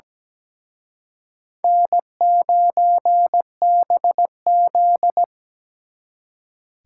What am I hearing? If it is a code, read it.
Morse code, heard as a keyed tone, about 20 wpm, 700 Hz. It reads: N9BZ
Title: Speed Racing - USA 1x2 Call Signs 17wpm